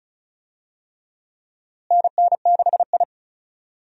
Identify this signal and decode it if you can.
Morse code, heard as a keyed tone, about 34 wpm, 700 Hz. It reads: NN6I
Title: USA Call Signs - 4 Characters - Repeated 3x - 35wpm